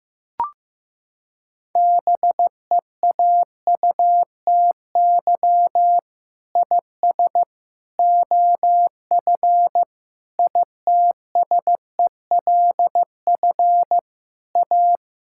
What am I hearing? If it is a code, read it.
Morse code, heard as a keyed tone, about 15 wpm, 700 Hz. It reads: BEAUTY IS OF ITSELF A
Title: Sentences from Top 1000 Words - Review Part 5 - 15wpm